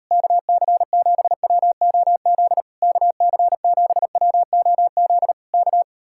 Morse code, heard as a keyed tone, about 37 wpm, 700 Hz. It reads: KC7WOZ KC7WOZ K